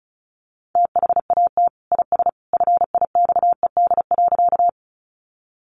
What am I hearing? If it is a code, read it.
Morse code, heard as a keyed tone, about 34 wpm, 700 Hz. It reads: THAT IS FIXED.
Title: Sentences from Top 1100 Words 35wpm